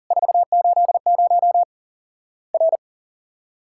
Morse code, heard as a keyed tone, about 40 wpm, 700 Hz. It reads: 480 R
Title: Worked All Europe DX Contest - QTC Exchange - 40wpm